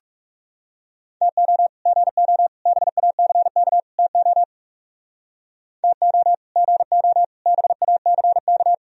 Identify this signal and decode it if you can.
Morse code, heard as a keyed tone, about 40 wpm, 700 Hz. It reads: TO GO BACK TO TO GO BACK